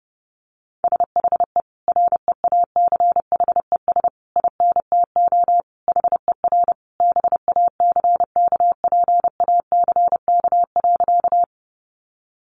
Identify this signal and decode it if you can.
Morse code, heard as a keyed tone, about 30 wpm, 700 Hz. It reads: SHE REACHES INTO HER BACKPACK.
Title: Additional Easy Phrases 30wpm